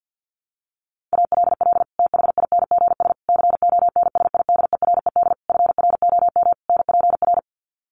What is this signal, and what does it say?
Morse code, heard as a keyed tone, about 50 wpm, 700 Hz. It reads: ALL THINGS CONSIDERED FROM NPR
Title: Sets of 5 Words Encore 50wpm